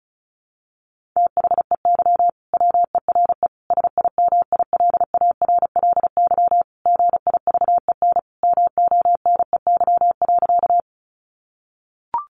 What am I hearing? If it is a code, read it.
Morse code, heard as a keyed tone, about 34 wpm, 700 Hz. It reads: THEY WERE SIMILARLY GIVEN MONEY.
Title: Sentences from Top 1400 Words 35wpm